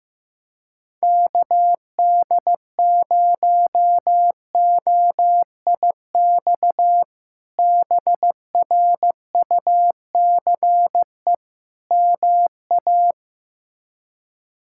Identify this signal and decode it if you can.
Morse code, heard as a keyed tone, about 15 wpm, 700 Hz. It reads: KD0OIX BRUCE MA
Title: K1USN SST QSO Practice - 15wpm